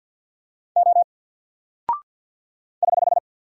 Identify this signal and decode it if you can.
Morse code, heard as a keyed tone, about 48 wpm, 700 Hz. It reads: O <HH>
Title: Single Character (Letter:Number:Punctuation:Prosign) Alt 50wpm